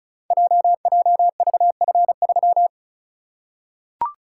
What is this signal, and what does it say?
Morse code, heard as a keyed tone, about 34 wpm, 690 Hz. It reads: JJVF3
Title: Code Groups - 5 Letter:Numbers - 35wpm